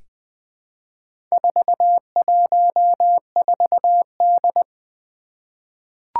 Morse code, heard as a keyed tone, about 20 wpm, 700 Hz. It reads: 414D
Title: Four Challenging Letter-Number Combinations 20wpm